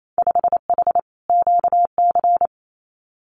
Morse code, heard as a keyed tone, about 28 wpm, 700 Hz. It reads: 5H QC
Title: Winter Field Day Exchanges Encore 28wpm